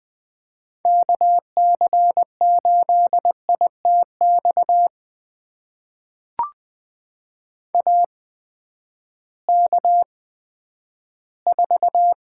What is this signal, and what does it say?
Morse code, heard as a keyed tone, about 20 wpm, spaced as for 11 wpm, 700 Hz. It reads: KC8ITX AK4
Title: USA Call Signs - 8x Character Spacing - 20wpm